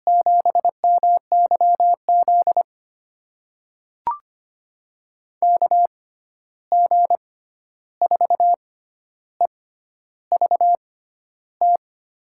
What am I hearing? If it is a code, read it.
Morse code, heard as a keyed tone, about 25 wpm, spaced as for 11 wpm, 700 Hz. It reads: 7MYZ KG4EVT
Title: USA Call Signs - 6 Characters - 6x Character Spacing - 25wpm